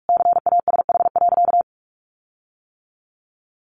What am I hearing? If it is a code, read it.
Morse code, heard as a keyed tone, about 44 wpm, 700 Hz. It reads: CASH.
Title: Sentences from Top 1300 Words 45wpm